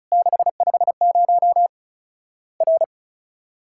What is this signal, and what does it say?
Morse code, heard as a keyed tone, about 36 wpm, 700 Hz. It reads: 650 R
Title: Worked All Europe DX Contest - QTC Exchange - 35wpm